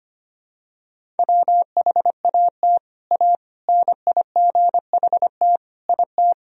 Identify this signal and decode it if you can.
Morse code, heard as a keyed tone, about 25 wpm, 700 Hz. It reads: WHAT A NIGHT IT
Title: Sentences from Top 300 Words and Abbreviations 25wpm